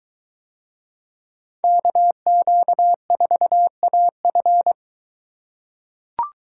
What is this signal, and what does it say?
Morse code, heard as a keyed tone, about 21 wpm, 700 Hz. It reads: KQ4AF